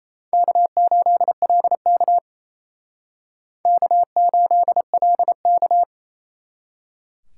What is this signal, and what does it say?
Morse code, heard as a keyed tone, about 35 wpm, 700 Hz. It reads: K8LK K8LK
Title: Speed Racing - USA Call Signs 28wpm